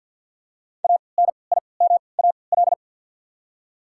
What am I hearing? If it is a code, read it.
Morse code, heard as a keyed tone, about 53 wpm, spaced as for 29 wpm, 700 Hz. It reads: ANIMAL
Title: 3x Character Spacing - Top 500 Words - 50wpm